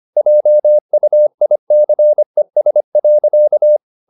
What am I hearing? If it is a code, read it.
Morse code, heard as a keyed tone, about 25 wpm, 600 Hz. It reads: JUICES.
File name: the old man and the sea part 1 25wpm